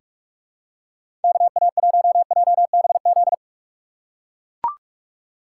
Morse code, heard as a keyed tone, about 43 wpm, 700 Hz. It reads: KA1JBZ
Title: USA Call Signs - 4x Character Spacing - 45wpm